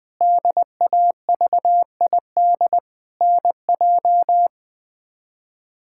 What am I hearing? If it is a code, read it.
Morse code, heard as a keyed tone, about 20 wpm, 700 Hz. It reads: DAVID NJ
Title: K1USN SST QSO Practice - 20wpm